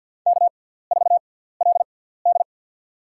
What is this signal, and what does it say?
Morse code, heard as a keyed tone, about 47 wpm, spaced as for 17 wpm, 700 Hz. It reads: K4LD
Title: USA Call Signs - 4 Characters - 6x Character Spacing - 50wpm